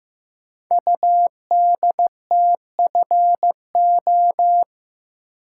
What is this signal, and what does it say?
Morse code, heard as a keyed tone, about 15 wpm, 700 Hz. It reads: UDTFO
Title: Code Groups - 5 Letters - 15wpm